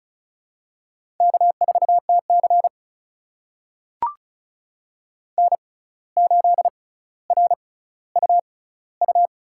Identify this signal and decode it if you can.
Morse code, heard as a keyed tone, about 36 wpm, spaced as for 17 wpm, 700 Hz. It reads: K4TC N8RUU
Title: USA Call Signs - 6x Character Spacing - 35wpm